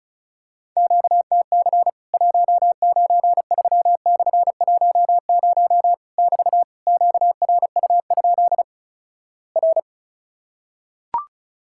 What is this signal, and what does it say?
Morse code, heard as a keyed tone, about 36 wpm, 700 Hz. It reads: QTC 193/10 <BT> QRU? R
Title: Worked All Europe DX Contest - QTC Exchange - 35wpm